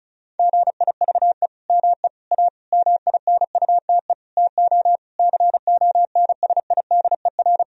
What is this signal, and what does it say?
Morse code, heard as a keyed tone, about 36 wpm, 700 Hz. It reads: GIVE ME A MINUTE TO CONSIDER
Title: Sentences from Top 1000 Words - Review Part 4 - 35wpm